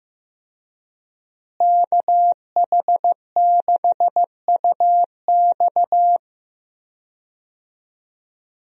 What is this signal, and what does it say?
Morse code, heard as a keyed tone, about 16 wpm, 700 Hz. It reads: KH6UX
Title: USA Call Signs - 5 Characters - Repeated 3x - 15wpm